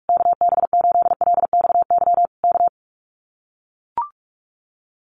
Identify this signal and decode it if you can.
Morse code, heard as a keyed tone, about 44 wpm, 700 Hz. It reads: KB8LXY K
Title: Common QSO Phrases Encore 45wpm